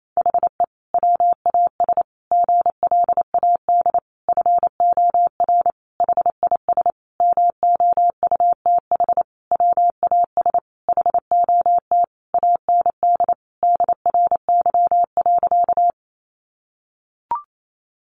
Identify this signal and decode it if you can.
Morse code, heard as a keyed tone, about 28 wpm, 700 Hz. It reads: HE WAS GLAD FOR HIS MOUTH WAS HOT AND DRY.
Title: Sentences from Top 1500 Words 28wpm